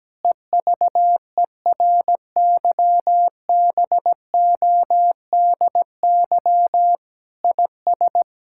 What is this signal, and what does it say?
Morse code, heard as a keyed tone, about 17 wpm, 700 Hz. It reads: EVERYBODY IS
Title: Sentences from Top 1600 Words 17wpm